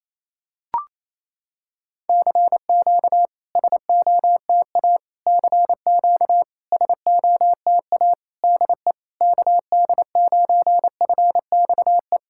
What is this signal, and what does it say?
Morse code, heard as a keyed tone, about 28 wpm, 700 Hz. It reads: CQ SOTA CQ SOTA DE KD9FXE
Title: SOTA QSO Practice 28wpm